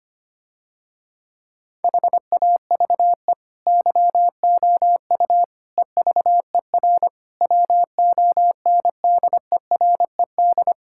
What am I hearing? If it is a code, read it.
Morse code, heard as a keyed tone, about 25 wpm, 700 Hz. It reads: HAVE YOU EVER WONDERED
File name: practice / 25wpm / Difficult Sentences with Hints - 25wpm